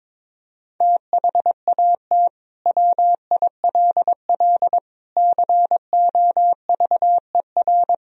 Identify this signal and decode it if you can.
Morse code, heard as a keyed tone, about 22 wpm, 700 Hz. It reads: THAT WILL COVER
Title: Sentences from Top 1000 Words - Review Part 1 - 22wpm